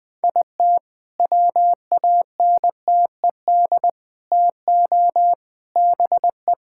Morse code, heard as a keyed tone, about 20 wpm, 700 Hz. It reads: IT WANTED TO BE
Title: All-Age Appropriate Jokes - 20wpm